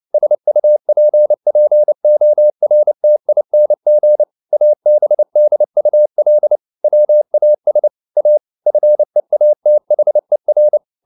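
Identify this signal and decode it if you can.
Morse code, heard as a keyed tone, about 29 wpm, 600 Hz. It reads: SUPPORTING ABDUL WAS A FEATHER
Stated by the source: Morse code at 29 wpm, sending Ken Follett's novel Never in English, from CW med Gnister.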